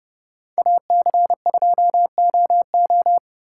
Morse code, heard as a keyed tone, about 28 wpm, 700 Hz. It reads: AC2OO